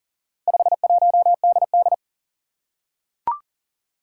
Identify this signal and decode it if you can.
Morse code, heard as a keyed tone, about 40 wpm, 700 Hz. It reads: 51DD
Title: Four Challenging Letter-Number Combinations 40wpm